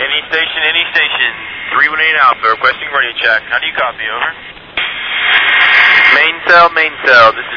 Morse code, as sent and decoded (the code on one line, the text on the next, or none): none